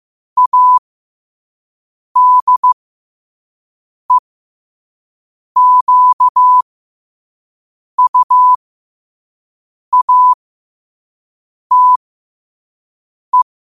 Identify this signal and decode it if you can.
Morse code, heard as a keyed tone, about 15 wpm, spaced as for 5 wpm, 1000 Hz. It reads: ADEQUATE
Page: New quotes every day in morse code at 5 Words per minute.